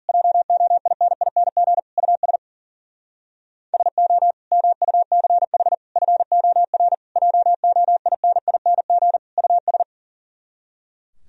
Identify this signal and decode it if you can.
Morse code, heard as a keyed tone, about 51 wpm, 700 Hz. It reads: JOINING US SO MUCH FOR JOINING US